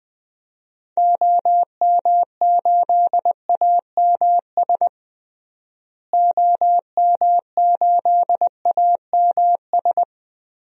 Morse code, heard as a keyed tone, about 20 wpm, 700 Hz. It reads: OM8AMS OM8AMS
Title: DXCC-100 Call Signs - 6 Characters - Repeated 3x - 20wpm